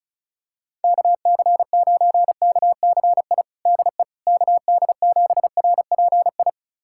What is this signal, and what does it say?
Morse code, heard as a keyed tone, about 36 wpm, 700 Hz. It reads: KC9KCI DE KD7RPI
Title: USA Callsign DE Callsign 35wpm